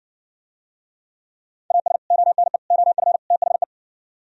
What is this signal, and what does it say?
Morse code, heard as a keyed tone, about 68 wpm, 700 Hz. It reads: AS ONE OF THE